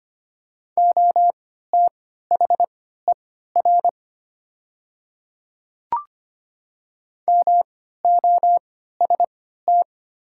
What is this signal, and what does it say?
Morse code, heard as a keyed tone, about 25 wpm, spaced as for 14 wpm, 700 Hz. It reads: OTHER MOST
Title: 3x Character Spacing - Top 100 Words - 25wpm